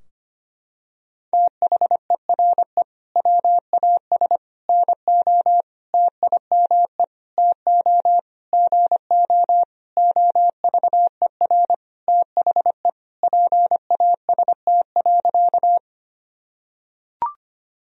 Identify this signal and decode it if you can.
Morse code, heard as a keyed tone, about 25 wpm, 700 Hz. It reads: THERE WAS NO TIME TO GO OVER THE PAST.
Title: Sentences from Top 1000 Words - Review Part 3 - 25wpm